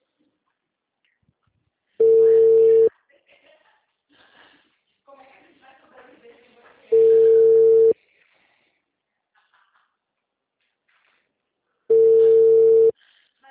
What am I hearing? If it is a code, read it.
Ringback tone (China, 450 Hz), with speech.